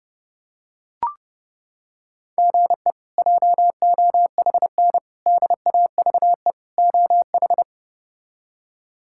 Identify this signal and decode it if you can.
Morse code, heard as a keyed tone, about 32 wpm, 700 Hz. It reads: GE JOHN DAVE OH
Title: K1USN SST QSO Practice - 30wpm